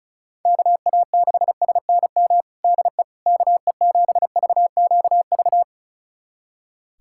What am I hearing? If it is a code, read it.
Morse code, heard as a keyed tone, about 34 wpm, 700 Hz. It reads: KA6SNM DE KE7VQV
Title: USA Callsign DE Callsign - 6 Characters - 35wpm